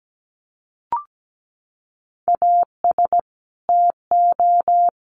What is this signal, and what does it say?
Morse code, heard as a keyed tone, about 17 wpm, 700 Hz. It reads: AS TO
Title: Sets of 5 Words Encore 17wpm